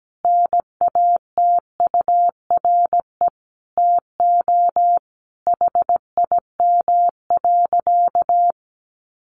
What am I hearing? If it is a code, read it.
Morse code, heard as a keyed tone, about 17 wpm, 700 Hz. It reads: NATURE TO HIM.
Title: Sentences from Top 600 Words 17wpm